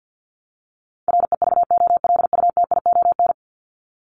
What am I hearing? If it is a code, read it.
Morse code, heard as a keyed tone, about 50 wpm, 700 Hz. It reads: REVOLUTION
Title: Top 1751-2000 Words 50wpm